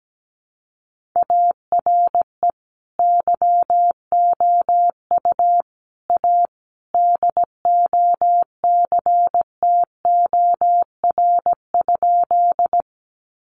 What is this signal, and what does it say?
Morse code, heard as a keyed tone, about 17 wpm, 700 Hz. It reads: ARE YOU A DOCTOR?
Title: Sentences from Top 1500 Words 17wpm